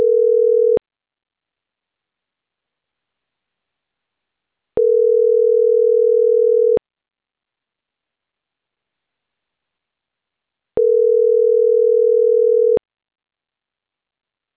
Ringback tone (North America), 440 + 480 Hz.